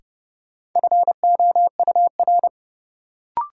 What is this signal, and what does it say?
Morse code, heard as a keyed tone, about 30 wpm, 700 Hz. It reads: FOUR
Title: Sentences from Top 300 Words - New Word Drill - 30wpm